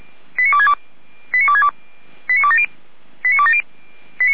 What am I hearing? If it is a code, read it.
Five-tone selective call (ZVEI), heard as quick runs of five tones, tones of about 70 ms. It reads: 89262 89262 89270